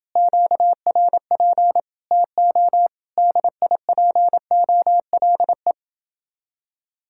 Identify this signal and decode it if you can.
Morse code, heard as a keyed tone, about 29 wpm, 700 Hz. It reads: QRP TO DIPOLE